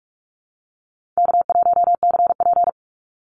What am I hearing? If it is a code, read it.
Morse code, heard as a keyed tone, about 44 wpm, 700 Hz. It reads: K1CP
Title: Easy Call Signs Repeated 3x - 45wpm